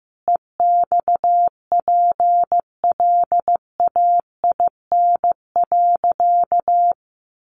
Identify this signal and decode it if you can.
Morse code, heard as a keyed tone, about 15 wpm, 700 Hz. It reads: EXPLAIN.
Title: Sentences from Top 1100 Words 15wpm